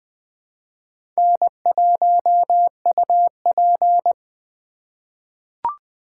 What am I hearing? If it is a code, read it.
Morse code, heard as a keyed tone, about 20 wpm, 700 Hz. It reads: N1UP
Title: USA Call Signs - 4 Characters - 2x Character Spacing - 20wpm